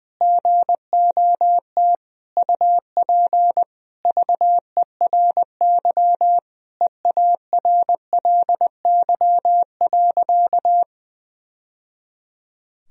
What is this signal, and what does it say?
Morse code, heard as a keyed tone, about 21 wpm, 700 Hz. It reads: GOT UP VERY EARLY.
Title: Sentences from Top 1000 Words - Review Part 2 - 20wpm